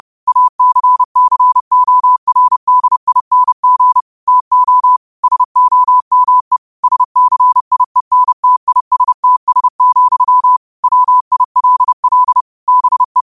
Morse code, heard as a keyed tone, about 30 wpm, 1000 Hz. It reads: ACCORDING TO SOME SCIENTISTS, WILL BE